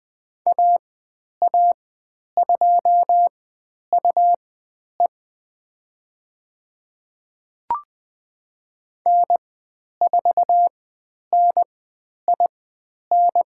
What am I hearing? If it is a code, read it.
Morse code, heard as a keyed tone, about 20 wpm, spaced as for 10 wpm, 700 Hz. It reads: AA2UE N4NIN
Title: 102 Lesson 17g - Letter u - Call Signs 20wpm